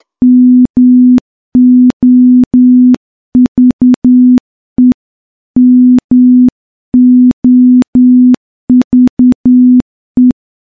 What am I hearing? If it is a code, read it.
Morse code, heard as a keyed tone, about 10 wpm, 250 Hz. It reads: MOVE MOVE
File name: morse-training-10WPM